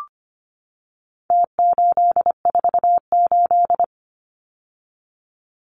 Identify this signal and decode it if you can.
Morse code, heard as a keyed tone, about 25 wpm, 700 Hz. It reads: T848
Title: Numbers - Contest Style - 4-Digit 25wpm